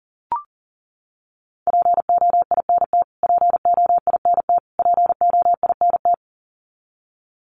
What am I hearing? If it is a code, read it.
Morse code, heard as a keyed tone, about 38 wpm, 700 Hz. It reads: POINT POINT POINT
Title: Top 200 Words Repeated 40wpm